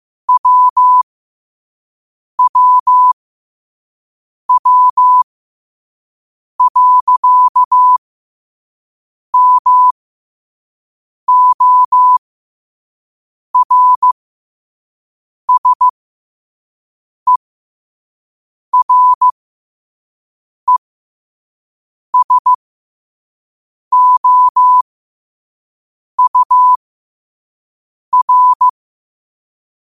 Morse code, heard as a keyed tone, about 15 wpm, spaced as for 5 wpm, 1000 Hz. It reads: WWW.MORSERESOUR
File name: qotdfiles / 5wpm